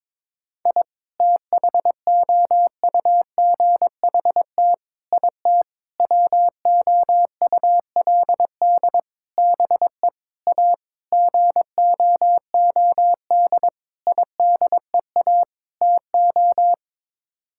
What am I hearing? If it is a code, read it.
Morse code, heard as a keyed tone, about 22 wpm, 700 Hz. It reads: I THOUGHT IT WOULD BE A GOOD IDEA TO
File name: Sets of 9 Words - 22wpm